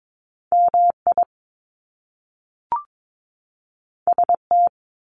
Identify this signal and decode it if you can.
Morse code, heard as a keyed tone, about 22 wpm, 700 Hz. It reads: MI ST
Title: Top Two Letter Combinations in Words 22wpm